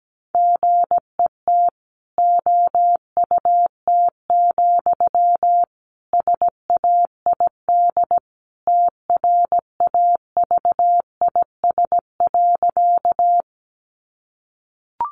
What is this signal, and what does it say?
Morse code, heard as a keyed tone, about 17 wpm, 700 Hz. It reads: GET OUT, SAID TRAVIS.
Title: A Sound Of Thunder 17wpm